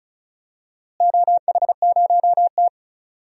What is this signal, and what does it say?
Morse code, heard as a keyed tone, about 34 wpm, 700 Hz. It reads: OH0T
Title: DXCC-100 Call Signs - 4 Characters - Repeated 2x - 35wpm